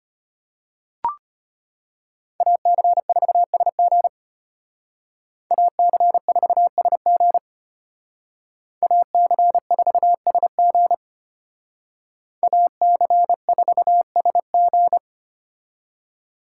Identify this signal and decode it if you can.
Morse code, heard as a keyed tone, about 32 wpm, 700 Hz. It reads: AC4SG AC4SG AC4SG AC4SG